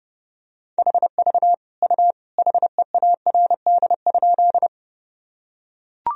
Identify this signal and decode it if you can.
Morse code, heard as a keyed tone, about 30 wpm, 700 Hz. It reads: HV U HEARD?
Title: Sentences from Top 400 Words and Abbreviations 30wpm